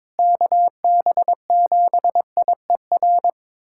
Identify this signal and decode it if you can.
Morse code, heard as a keyed tone, about 22 wpm, 700 Hz. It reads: KB7IER
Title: USA Callsign DE Callsign - 6 Characters - 22wpm